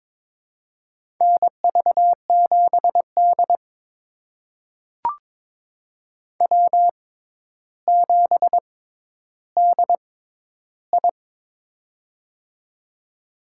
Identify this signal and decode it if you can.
Morse code, heard as a keyed tone, about 22 wpm, spaced as for 11 wpm, 700 Hz. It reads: NV7D W7DI
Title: USA Call Signs - 4 Characters - 6x Character Spacing - 22wpm